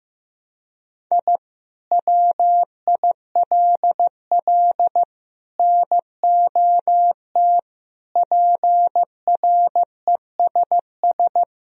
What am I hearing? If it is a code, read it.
Morse code, heard as a keyed tone, about 15 wpm, 700 Hz. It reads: I WILL NOT PRESS
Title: Sentences from Top 1000 Words - Review Part 1 - 15wpm